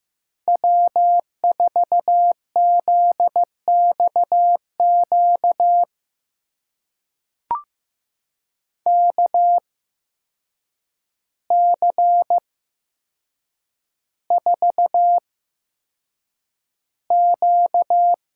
Morse code, heard as a keyed tone, about 15 wpm, spaced as for 7 wpm, 700 Hz. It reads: W4ZXQ KC4Q
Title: USA Call Signs - 5 Characters - 8x Character Spacing - 15wpm